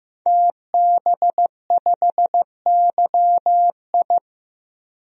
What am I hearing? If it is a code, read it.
Morse code, heard as a keyed tone, about 15 wpm, 700 Hz. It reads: TB5YI
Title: DXCC-100 Call Signs - 5 Characters - 15wpm